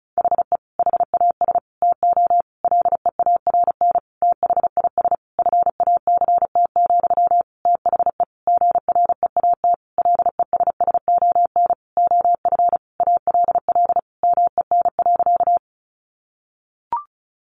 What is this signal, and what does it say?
Morse code, heard as a keyed tone, about 34 wpm, 700 Hz. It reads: HE HAS TO LEARN THIS FACT, THE GREAT LESSON OF ALL MEN.
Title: Sentences from Top 1200 Words 35wpm